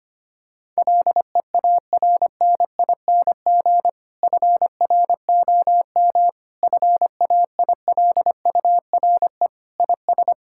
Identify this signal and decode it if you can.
Morse code, heard as a keyed tone, about 25 wpm, 700 Hz. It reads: LEARNING FROM FAILURE IS
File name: practice / 25wpm / Difficult Sentences with Hints - 25wpm